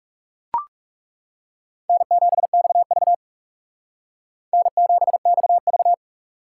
Morse code, heard as a keyed tone, about 36 wpm, 700 Hz. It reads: N7XV N7XV
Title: Speed Racing - USA Call Signs 30wpm